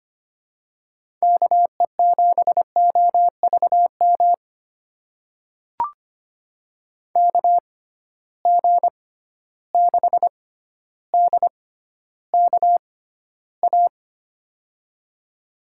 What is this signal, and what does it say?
Morse code, heard as a keyed tone, about 25 wpm, spaced as for 13 wpm, 700 Hz. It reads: KE7OVM KG6DKA
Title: USA Call Signs - 6 Characters - 6x Character Spacing - 25wpm